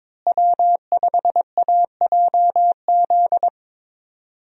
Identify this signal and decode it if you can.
Morse code, heard as a keyed tone, about 22 wpm, 700 Hz. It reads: W5AJZ
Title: USA Call Signs - 5 Characters - Repeated 3x - 22wpm